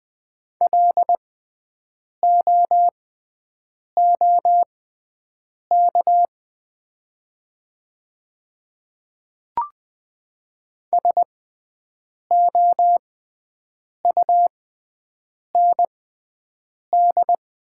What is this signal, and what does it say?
Morse code, heard as a keyed tone, about 20 wpm, spaced as for 7 wpm, 700 Hz. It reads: LOOK SOUND
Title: 6x Character Spacing - Top 100 Words - 20wpm